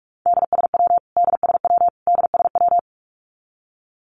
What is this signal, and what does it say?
Morse code, heard as a keyed tone, about 44 wpm, 700 Hz. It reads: DSW DSW DSW
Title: Abbreviations Repeated 3x - 45wpm